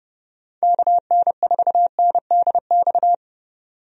Morse code, heard as a keyed tone, about 31 wpm, 700 Hz. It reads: KN4NDX
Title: USA Call Signs - 6 Characters - 30wpm